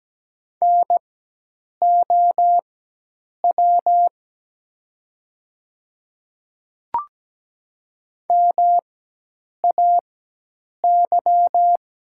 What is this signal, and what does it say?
Morse code, heard as a keyed tone, about 17 wpm, spaced as for 8 wpm, 700 Hz. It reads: NOW MAY
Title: 4x Character Spacing - Top 200 Words - 17wpm